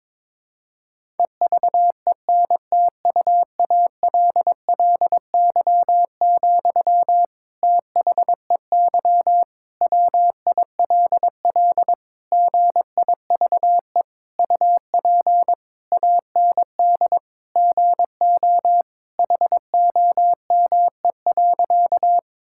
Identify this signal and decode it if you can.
Morse code, heard as a keyed tone, about 22 wpm, 700 Hz. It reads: EVENTUALLY, THEY WILL GIVE UP AND GO HOME.
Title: Sentences from Top 1000 Words - Review Part 6 - 22wpm